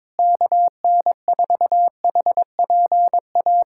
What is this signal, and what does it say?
Morse code, heard as a keyed tone, about 21 wpm, 700 Hz. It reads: KN4HPA
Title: USA Callsign DE Callsign - 6 Characters - 22wpm